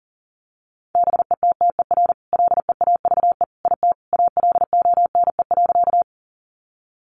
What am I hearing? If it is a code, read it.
Morse code, heard as a keyed tone, about 37 wpm, 700 Hz. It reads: BETTER LEAVE IT ALONE.
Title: Sentences from Top 1000 Words 40wpm